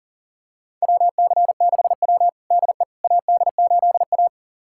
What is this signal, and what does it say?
Morse code, heard as a keyed tone, about 37 wpm, 700 Hz. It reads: WC6W DE AD8A
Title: USA Callsign DE Callsign - 4 Characters - 40wpm